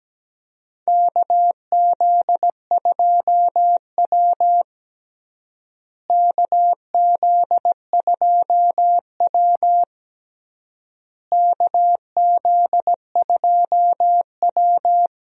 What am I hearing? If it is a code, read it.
Morse code, heard as a keyed tone, about 17 wpm, 700 Hz. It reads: KZ2W KZ2W KZ2W
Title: USA Call Signs - 4 Characters - Repeated 3x - 17wpm